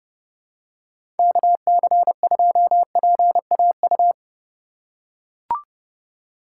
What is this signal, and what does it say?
Morse code, heard as a keyed tone, about 31 wpm, 700 Hz. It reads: KC2PAU
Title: USA Call Signs - 6 Characters - 3x Character Spacing - 30wpm